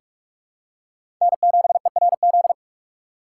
Morse code, heard as a keyed tone, about 42 wpm, 700 Hz. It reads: N7ERZ
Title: USA Call Signs - 5 Characters - 6x Character Spacing - 45wpm